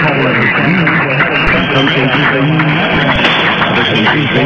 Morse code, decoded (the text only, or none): none